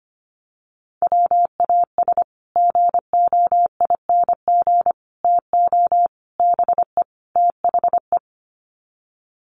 Morse code, heard as a keyed tone, about 25 wpm, 700 Hz. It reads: WAS GOING TO BE THE
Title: Sets of 5 Words Encore2 25wpm